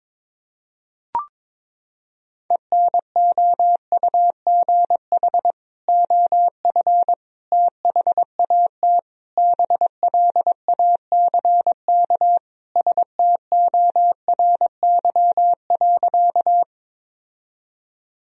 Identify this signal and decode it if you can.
Morse code, heard as a keyed tone, about 22 wpm, 700 Hz. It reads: ENOUGH OF THAT BLACK STORY.
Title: Sentences from Top 1000 Words - Review Part 2 - 22wpm